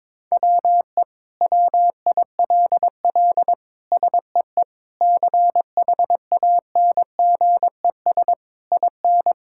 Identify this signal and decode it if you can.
Morse code, heard as a keyed tone, about 22 wpm, 700 Hz. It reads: WE WILL SEE CHANGES IN
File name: Sentences with 7-Word Phrases - 22wpm